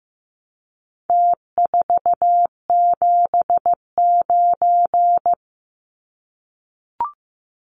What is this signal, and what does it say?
Morse code, heard as a keyed tone, about 15 wpm, 700 Hz. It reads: T479
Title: Numbers - Contest Style - 4-Digit 15wpm